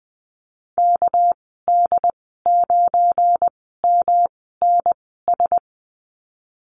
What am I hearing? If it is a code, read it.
Morse code, heard as a keyed tone, about 20 wpm, spaced as for 14 wpm, 700 Hz. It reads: KD9MNS
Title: 245 Lesson 39g - Letter z - Call Signs 20wpm